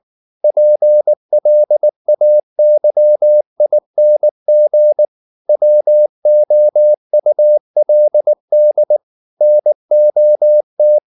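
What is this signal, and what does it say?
Morse code, heard as a keyed tone, about 19 wpm, 600 Hz. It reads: PLAYING WOULD NOT
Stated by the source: Morse code at 19 wpm from CW med Gnister, reading Ken Follett's novel Never in English.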